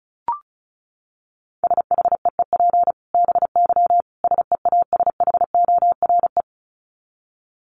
Morse code, heard as a keyed tone, about 34 wpm, 700 Hz. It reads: SHEEP BY SEASHORE
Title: Dit Challenge 35wpm